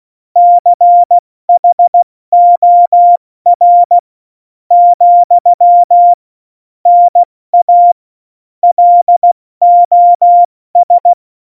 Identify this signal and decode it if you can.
Morse code, heard as a keyed tone, about 16 wpm, spaced as for 13 wpm, 700 Hz. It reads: CHOR, NA LOS
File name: Morsekurs / Witze / Witz 03.Juli 201, 70 F80